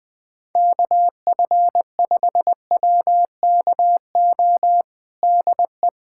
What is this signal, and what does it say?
Morse code, heard as a keyed tone, about 21 wpm, 700 Hz. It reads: KF5WKO DE
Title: USA Callsign DE Callsign 20wpm